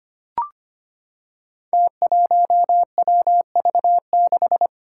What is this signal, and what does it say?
Morse code, heard as a keyed tone, about 25 wpm, 700 Hz. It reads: T1WV6
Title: Five Challenging Letter-Number Combinations 25wpm